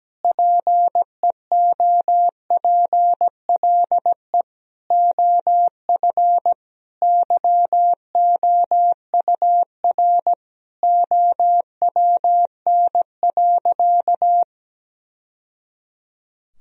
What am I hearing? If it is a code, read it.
Morse code, heard as a keyed tone, about 17 wpm, 700 Hz. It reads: PEOPLE OF YOUR OWN.
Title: Sentences from Top 1000 Words - Review Part 2 - 17wpm